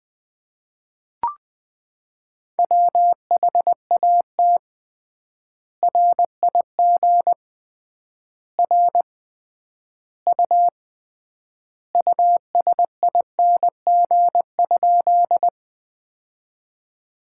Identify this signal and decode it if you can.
Morse code, heard as a keyed tone, about 20 wpm, 700 Hz. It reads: WHAT RIG R U USING?
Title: Common QSO Phrases - 3x Word-Spacing - 20wpm